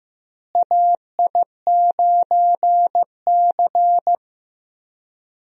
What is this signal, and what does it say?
Morse code, heard as a keyed tone, about 15 wpm, 700 Hz. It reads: AI9C
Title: USA Call Signs - 4 Characters - Repeated 3x - 15wpm